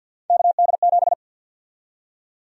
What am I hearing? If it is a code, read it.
Morse code, heard as a keyed tone, about 49 wpm, 700 Hz. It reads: KD7
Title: USA Call Sign Prefixes - 50wpm